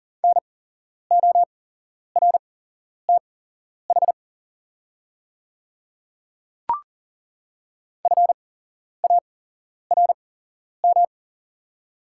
Morse code, heard as a keyed tone, about 40 wpm, spaced as for 11 wpm, 690 Hz. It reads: NORTH FARM
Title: 8x Character Spacing - Top 500 Words - 40wpm